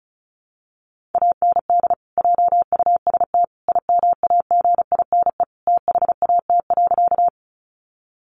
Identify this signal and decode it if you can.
Morse code, heard as a keyed tone, about 34 wpm, 700 Hz. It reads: AND JUST IMAGINE THAT.
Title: Sentences from Top 1400 Words 35wpm